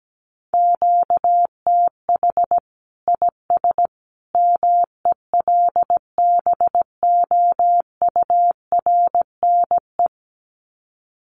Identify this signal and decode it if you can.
Morse code, heard as a keyed tone, about 17 wpm, 700 Hz. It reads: QTH IS MELBOURNE
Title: Common QSO Phrases 17wpm